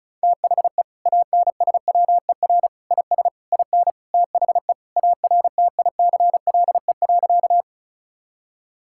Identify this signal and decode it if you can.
Morse code, heard as a keyed tone, about 36 wpm, 700 Hz. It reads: THE ANSWER IS IN THE ARTICLE.
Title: Sentences from Top 1000 Words - Review Part 5 - 35wpm